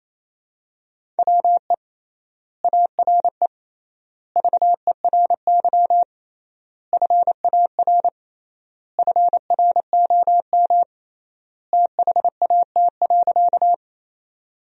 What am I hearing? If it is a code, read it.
Morse code, heard as a keyed tone, about 28 wpm, 700 Hz. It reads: WE ARE VERY FAR FROM THAT.
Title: 3x Word-Spacing - Sentences from Top 300 Words - 28wpm